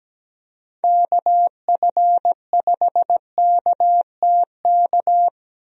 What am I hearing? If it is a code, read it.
Morse code, heard as a keyed tone, about 17 wpm, 700 Hz. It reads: KF5KTK
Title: USA Call Signs - 6 Characters - Repeated 3x - 17wpm